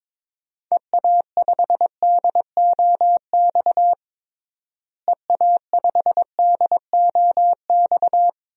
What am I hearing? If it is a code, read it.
Morse code, heard as a keyed tone, about 22 wpm, 700 Hz. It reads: EA5DOX EA5DOX
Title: DXCC-100 Call Signs - 6 Characters - Repeated 2x - 22wpm